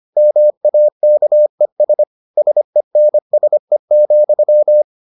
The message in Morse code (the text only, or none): MAKES SENSE,